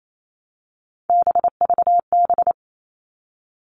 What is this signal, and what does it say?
Morse code, heard as a keyed tone, about 28 wpm, 700 Hz. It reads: BVB
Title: All Cut Numbers - 3-Digit 28wpm